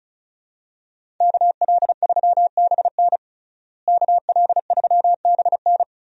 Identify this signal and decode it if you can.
Morse code, heard as a keyed tone, about 36 wpm, 700 Hz. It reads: KL3BN KL3BN
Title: USA Call Signs - 5 Characters - Repeated 3x - 35wpm